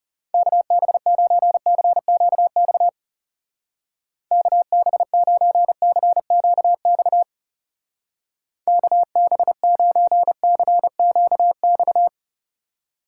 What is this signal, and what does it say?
Morse code, heard as a keyed tone, about 39 wpm, 700 Hz. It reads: KB9CQX KB9CQX KB9CQX